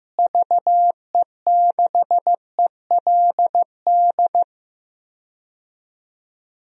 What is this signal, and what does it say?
Morse code, heard as a keyed tone, about 16 wpm, 700 Hz. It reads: VE6ELD
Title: DXCC-100 Call Signs - 6 Characters - Repeated 3x - 15wpm